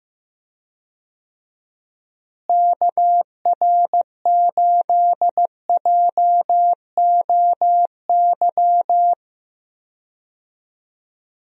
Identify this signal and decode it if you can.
Morse code, heard as a keyed tone, about 15 wpm, 700 Hz. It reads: KR8JOY
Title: USA Call Signs - 6 Characters - Repeated 3x - 15wpm